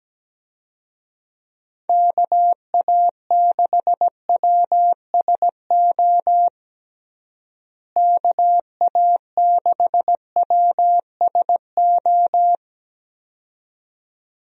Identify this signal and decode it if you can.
Morse code, heard as a keyed tone, about 17 wpm, 700 Hz. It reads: KA6WSO KA6WSO
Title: USA Call Signs - 6 Characters - Repeated 2x - 17wpm